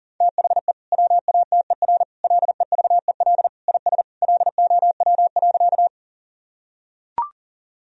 Morse code, heard as a keyed tone, about 40 wpm, 700 Hz. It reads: THE WATER LEVEL IS LOW.
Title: Sentences from Top 1000 Words - Review Part 4 - 40wpm